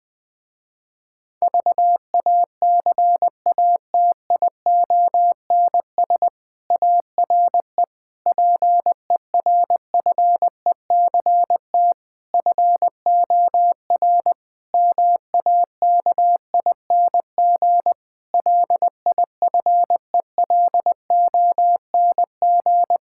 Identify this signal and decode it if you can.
Morse code, heard as a keyed tone, about 20 wpm, 700 Hz. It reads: VACATIONS ARE PERFECT FOR MAKING LIFELONG
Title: Difficult Sentences with Hints - 20wpm